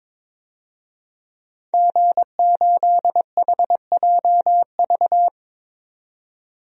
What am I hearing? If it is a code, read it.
Morse code, heard as a keyed tone, about 22 wpm, 700 Hz. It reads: G8HJV
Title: DXCC-100 Call Signs - 5 Characters - 22wpm